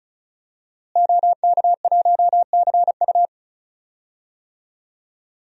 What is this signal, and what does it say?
Morse code, heard as a keyed tone, about 36 wpm, 700 Hz. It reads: OK1CU
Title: DXCC-100 Call Signs - 5 Characters - Repeated 2x - 35wpm